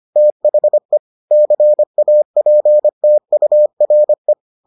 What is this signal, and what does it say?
Morse code, heard as a keyed tone, about 25 wpm, 600 Hz. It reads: THE CAPTURE